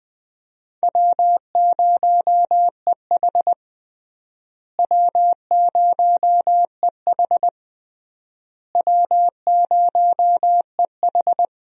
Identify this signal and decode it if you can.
Morse code, heard as a keyed tone, about 20 wpm, 700 Hz. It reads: W0EH W0EH W0EH
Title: USA Call Signs - 4 Characters - Repeated 3x - 20wpm